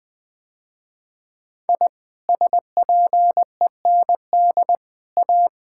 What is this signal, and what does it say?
Morse code, heard as a keyed tone, about 21 wpm, 700 Hz. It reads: I SPEND A
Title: Sentences with 5-Word Phrases - Spoken Beginning - 20wpm